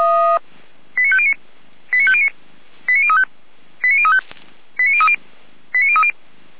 Five-tone selective call (ZVEI), heard as quick runs of five tones, tones of about 70 ms. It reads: 89509 89509 89035 89035 89030 89030